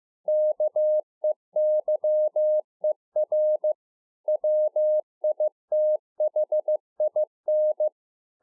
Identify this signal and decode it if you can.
Morse code, heard as a keyed tone, about 15 wpm, 610 Hz. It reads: KEYER WITHIN
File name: 2011-04-mp3 / A SMALL TRANCEIVER 2 15WPM